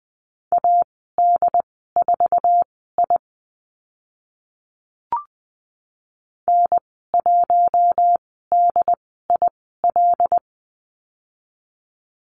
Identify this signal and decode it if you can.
Morse code, heard as a keyed tone, about 20 wpm, spaced as for 14 wpm, 700 Hz. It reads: AD4I N1DIL
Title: 082 Lesson 14g - Number 2 - Call Signs 20wpm